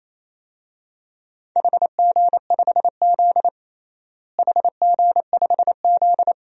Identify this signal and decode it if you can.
Morse code, heard as a keyed tone, about 28 wpm, 700 Hz. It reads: HG5Z HG5Z
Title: DXCC-100 Call Signs - 4 Characters - Repeated 2x - 28wpm